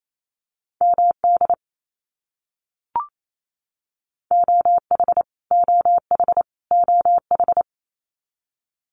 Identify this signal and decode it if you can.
Morse code, heard as a keyed tone, about 28 wpm, 700 Hz. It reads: MD OH OH OH
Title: US States Repeated 3x 28wpm